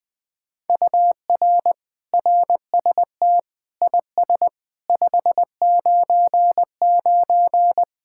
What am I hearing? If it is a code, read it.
Morse code, heard as a keyed tone, about 21 wpm, 700 Hz. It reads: UR RST IS 599
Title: Speed-Racing Common QSO Phrases - 20wpm